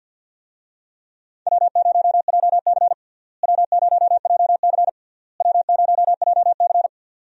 Morse code, heard as a keyed tone, about 47 wpm, 700 Hz. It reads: W0JC W0JC W0JC
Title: USA Call Signs - 4 Characters - Repeated 3x - 50wpm